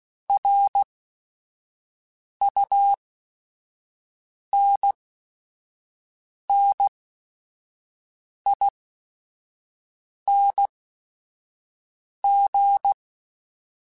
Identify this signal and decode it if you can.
Morse code, heard as a keyed tone, about 16 wpm, spaced as for 5 wpm, 800 Hz. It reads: RUNNING
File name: QSOs / QSOs-mp3s / WD0FBT 1@5